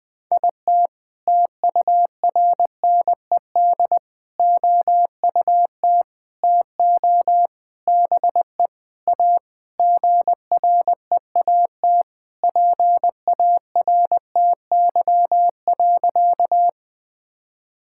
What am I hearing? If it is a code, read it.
Morse code, heard as a keyed tone, about 20 wpm, 700 Hz. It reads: IT TURNED OUT TO BE A GREAT PARTY.
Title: Sentences with 5-Word Phrases - 20wpm